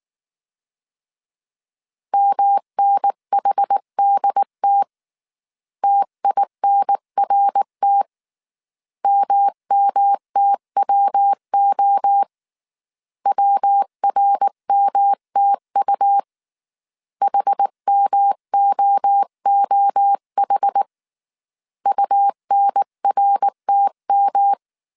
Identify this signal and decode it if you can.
Morse code, heard as a keyed tone, about 19 wpm, 800 Hz. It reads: MNHDT TINRT MMTWO WRMTU HMOOH UNRTM